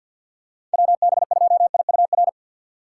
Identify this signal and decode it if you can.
Morse code, heard as a keyed tone, about 50 wpm, 700 Hz. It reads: WB2IUR
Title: USA Call Signs - 4x Character Spacing - 50wpm